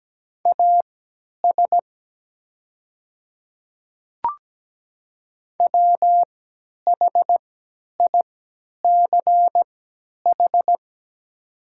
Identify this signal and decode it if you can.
Morse code, heard as a keyed tone, about 17 wpm, spaced as for 10 wpm, 700 Hz. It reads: AS WHICH
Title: 3x Character Spacing - Top 100 Words - 17wpm